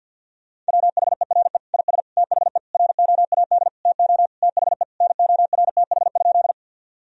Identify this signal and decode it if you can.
Morse code, heard as a keyed tone, about 50 wpm, 700 Hz. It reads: WHERE IS THE ROAD TO THE NORTH?
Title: Sentences from Top 1000 Words - Review Part 2 - 50wpm